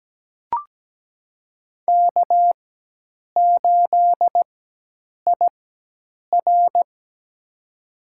Morse code, heard as a keyed tone, about 17 wpm, spaced as for 8 wpm, 700 Hz. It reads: K8IR
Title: USA Call Signs - 4 Characters - 4x Character Spacing - 17wpm